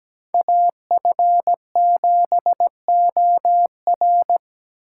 Morse code, heard as a keyed tone, about 17 wpm, 700 Hz. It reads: AF7OR
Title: USA Call Signs - 4x Character Spacing - 17wpm